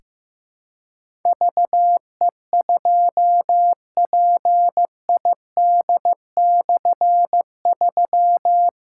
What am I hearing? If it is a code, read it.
Morse code, heard as a keyed tone, about 15 wpm, 700 Hz. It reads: VE2PID/3
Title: K1USN SST QSO Practice - 15wpm